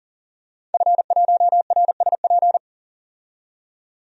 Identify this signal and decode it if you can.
Morse code, heard as a keyed tone, about 40 wpm, 700 Hz. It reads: F1RSP
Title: DXCC-100 Call Signs - 5 Characters - 40wpm